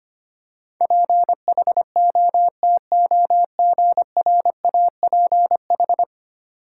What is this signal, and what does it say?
Morse code, heard as a keyed tone, about 25 wpm, 700 Hz. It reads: PHOTOGRAPH
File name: 10 Letter Words - First 3 Letters Given - 25wpm